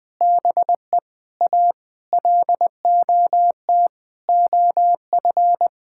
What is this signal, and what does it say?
Morse code, heard as a keyed tone, about 20 wpm, 700 Hz. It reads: BE A LOT OF
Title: Sentences with 8-Word Phrases - 20wpm